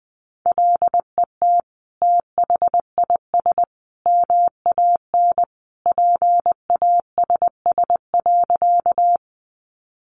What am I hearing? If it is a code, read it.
Morse code, heard as a keyed tone, about 20 wpm, 700 Hz. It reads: LET THIS MAN PASS.
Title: Sentences from Top 400 Words 20wpm